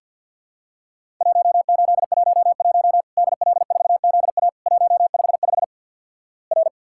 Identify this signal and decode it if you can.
Morse code, heard as a keyed tone, about 50 wpm, 700 Hz. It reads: 1811 DL4ZA 155 R
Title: Worked All Europe DX Contest - QTC Exchange - 50wpm